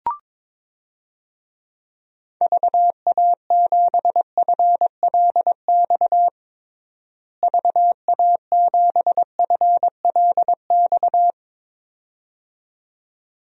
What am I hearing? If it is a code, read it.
Morse code, heard as a keyed tone, about 22 wpm, 700 Hz. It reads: VA7FLX VA7FLX
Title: DXCC-100 Call Signs - 6 Characters - Repeated 2x - 22wpm